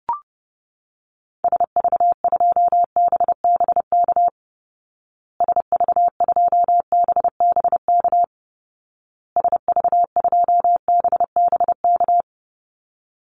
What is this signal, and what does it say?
Morse code, heard as a keyed tone, about 30 wpm, 700 Hz. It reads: SV2BBK SV2BBK SV2BBK
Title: Call Signs Repeated 2x - 30wpm